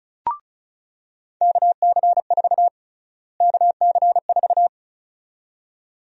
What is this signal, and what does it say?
Morse code, heard as a keyed tone, about 36 wpm, 700 Hz. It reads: KC4 KC4
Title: USA Call Sign Prefixes - Repeated 2x - 35wpm